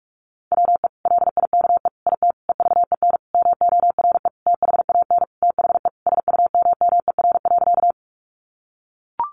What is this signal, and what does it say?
Morse code, heard as a keyed tone, about 44 wpm, 700 Hz. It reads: WE LIKE IT EVEN MORE THAN THE SUMMER.
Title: Sentences from Top 1100 Words 45wpm